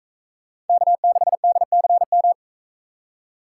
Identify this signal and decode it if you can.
Morse code, heard as a keyed tone, about 42 wpm, 700 Hz. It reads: K6DCM